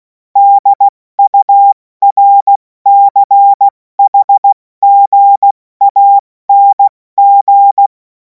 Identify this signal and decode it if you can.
Morse code, heard as a keyed tone, about 16 wpm, 800 Hz. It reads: DURCHGANG
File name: Morsekurs / Witze / Witz 31. März 2026 70 F80